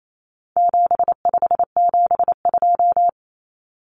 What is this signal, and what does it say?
Morse code, heard as a keyed tone, about 28 wpm, 700 Hz. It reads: 7572
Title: Numbers - 4-Digit 28wpm